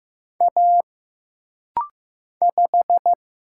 Morse code, heard as a keyed tone, about 15 wpm, 700 Hz. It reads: A 5
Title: Single Letter-Number Alt - Rapid-Fire 15wpm